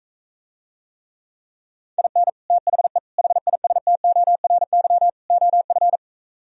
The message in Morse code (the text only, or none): IN THE HISTORY OF